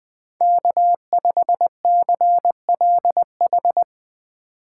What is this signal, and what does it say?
Morse code, heard as a keyed tone, about 20 wpm, 700 Hz. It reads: K5CLH
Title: Speed Racing - USA 1x3 Call Signs 20wpm